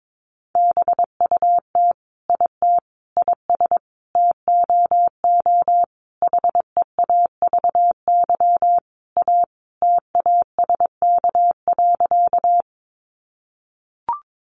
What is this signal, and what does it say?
Morse code, heard as a keyed tone, about 22 wpm, 700 Hz. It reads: BUT IT IS TOO HEAVY A TASK.
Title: Sentences from Top 1100 Words 22wpm